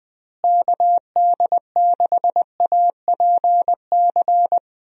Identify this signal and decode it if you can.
Morse code, heard as a keyed tone, about 20 wpm, 700 Hz. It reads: KD6APC
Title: USA Call Signs - 6 Characters - Repeated 2x - 20wpm